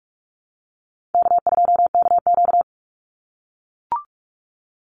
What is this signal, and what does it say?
Morse code, heard as a keyed tone, about 44 wpm, 700 Hz. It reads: K2KQ
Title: Call Signs - 45wpm